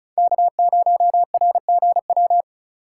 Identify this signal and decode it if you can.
Morse code, heard as a keyed tone, about 36 wpm, 700 Hz. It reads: K0RGW
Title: USA Call Signs - 5 Characters - 4x Character Spacing - 35wpm